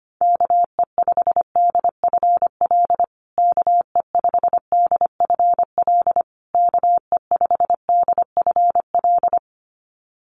Morse code, heard as a keyed tone, about 25 wpm, 700 Hz. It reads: KE5DFL KE5DFL KE5DFL
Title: USA Call Signs Repeated 3x 25wpm